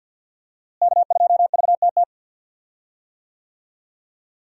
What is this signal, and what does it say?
Morse code, heard as a keyed tone, about 47 wpm, 700 Hz. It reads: K2VTT
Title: USA Call Signs - 5 Characters - Repeated 2x - 50wpm